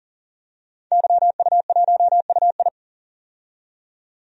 Morse code, heard as a keyed tone, about 40 wpm, 700 Hz. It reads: YU1UI
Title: DXCC-100 Call Signs - 5 Characters - 40wpm